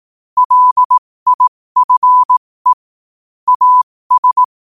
Morse code, heard as a keyed tone, about 18 wpm, spaced as for 15 wpm, 1000 Hz. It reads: LIFE AS
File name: qotdfiles / 15wpm